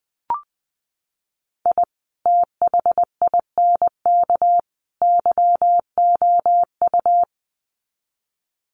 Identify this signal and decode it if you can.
Morse code, heard as a keyed tone, about 20 wpm, 700 Hz. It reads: I THINK YOU
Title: Sets of 3 Words 20wpm